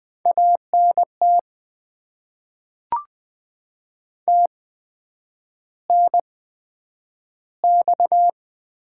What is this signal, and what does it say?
Morse code, heard as a keyed tone, about 20 wpm, spaced as for 9 wpm, 700 Hz. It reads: ANT TNX
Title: Top 20 QSO Elements - 8x Character Spacing - 20wpm